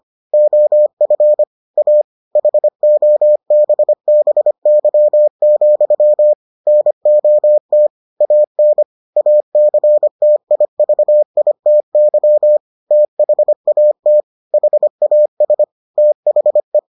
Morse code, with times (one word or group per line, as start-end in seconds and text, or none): 0.3-1.4 OF
1.8-2.0 A
2.4-6.3 HOBBY,
6.7-7.9 NOT
8.2-8.8 AN
9.2-12.6 ACTIVITY
12.9-14.2 THAT
14.5-15.6 HAS
16.0-16.8 THE